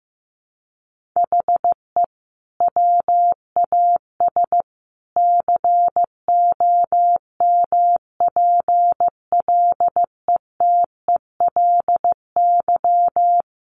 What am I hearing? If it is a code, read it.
Morse code, heard as a keyed tone, about 15 wpm, 700 Hz. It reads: HE WAS COMPLETELY
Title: Sentences from Top 1100 Words 15wpm